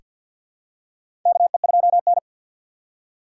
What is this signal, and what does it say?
Morse code, heard as a keyed tone, about 48 wpm, 700 Hz. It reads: KE2N
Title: USA Call Signs - 4 Characters - 2x Character Spacing - 50wpm